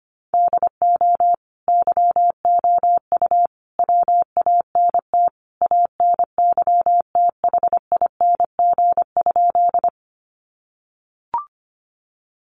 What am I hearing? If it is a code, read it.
Morse code, heard as a keyed tone, about 25 wpm, 700 Hz. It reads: DO YOU WANT ANYTHING?
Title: Sentences from Top 1000 Words 25wpm